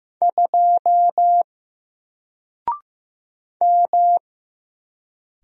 Morse code, heard as a keyed tone, about 15 wpm, 700 Hz. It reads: 2 M
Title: Single Letter-Number 15wpm